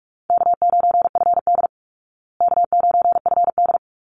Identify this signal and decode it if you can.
Morse code, heard as a keyed tone, about 44 wpm, 700 Hz. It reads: K9FD K9FD
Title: Call Signs Repeated 3x - 45wpm